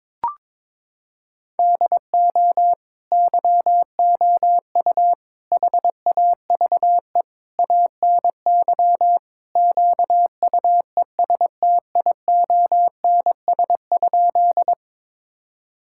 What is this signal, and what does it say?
Morse code, heard as a keyed tone, about 22 wpm, 700 Hz. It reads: DO YOU HAVE ANY QUESTIONS?
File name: practice / 22wpm / Sentences with 4-Word Phrases - 22wpm